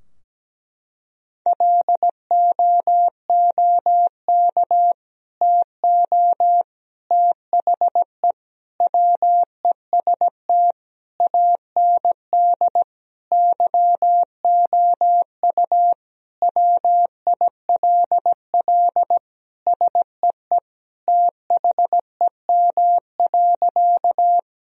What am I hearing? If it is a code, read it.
Morse code, heard as a keyed tone, about 17 wpm, 700 Hz. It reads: LOOK TO THE WEST AND YOU WILL SEE THEM.
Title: Sentences from Top 1000 Words - Review Part 2 - 17wpm